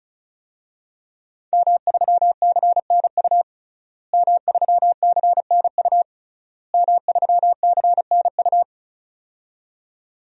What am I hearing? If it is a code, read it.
Morse code, heard as a keyed tone, about 34 wpm, 700 Hz. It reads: M3CNU M3CNU M3CNU
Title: DXCC-100 Call Signs - 5 Characters - Repeated 3x - 35wpm